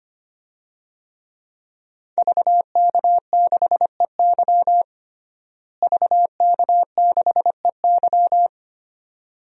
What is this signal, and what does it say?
Morse code, heard as a keyed tone, about 25 wpm, 700 Hz. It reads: VK6EY VK6EY
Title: DXCC-100 Call Signs - 5 Characters - Repeated 2x - 25wpm